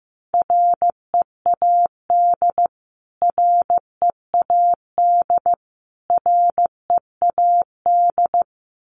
Morse code, heard as a keyed tone, about 15 wpm, 700 Hz. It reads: READ READ READ
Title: Top 500 Words Repeated 15wpm